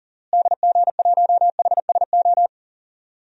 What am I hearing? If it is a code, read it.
Morse code, heard as a keyed tone, about 37 wpm, 700 Hz. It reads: DG1HSO
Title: DXCC-100 Call Signs - 6 Characters - Repeated 2x - 40wpm